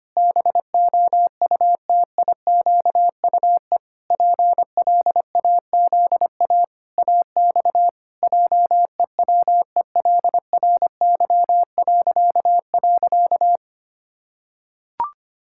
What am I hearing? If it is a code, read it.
Morse code, heard as a keyed tone, about 25 wpm, 700 Hz. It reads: BOUTIQUE PLAZA AX JEWELRY..
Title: Pangram Challenge v2 - 25wpm